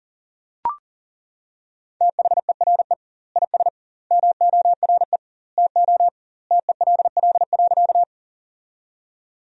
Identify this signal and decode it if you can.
Morse code, heard as a keyed tone, about 40 wpm, 700 Hz. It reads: THERE IS MORE TO TELL.
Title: 2x Word-Spacing - Sentences from Top 200 Words - 40wpm